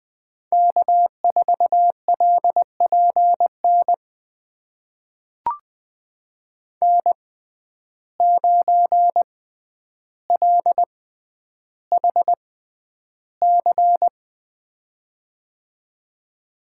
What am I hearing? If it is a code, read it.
Morse code, heard as a keyed tone, about 20 wpm, spaced as for 10 wpm, 700 Hz. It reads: K4LPN N9LHC
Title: USA Call Signs - 5 Characters - 6x Character Spacing - 20wpm